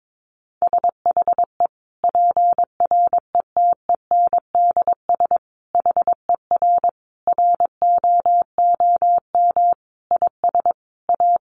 Morse code, heard as a keyed tone, about 22 wpm, 700 Hz. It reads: SHE PRETENDS HER ROOM IS A
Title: Additional Easy Phrases 22wpm